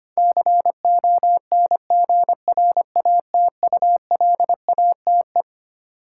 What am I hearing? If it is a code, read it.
Morse code, heard as a keyed tone, about 25 wpm, 700 Hz. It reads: CONGRATULATE
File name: Twelve Letter Words - 25wpm